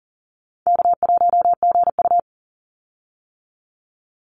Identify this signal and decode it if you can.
Morse code, heard as a keyed tone, about 40 wpm, 700 Hz. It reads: K1GU
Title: Call Signs - 40wpm